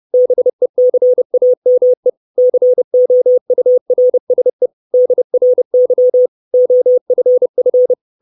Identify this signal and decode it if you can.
Morse code, heard as a keyed tone, about 30 wpm, 500 Hz. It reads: BECAME COURSE DRY OFF
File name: 632 Words 30WPM